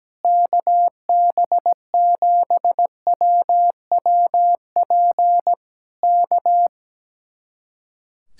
Morse code, heard as a keyed tone, about 17 wpm, 700 Hz. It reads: KB7WWP K
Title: Speed-Racing Common QSO Phrases - 17wpm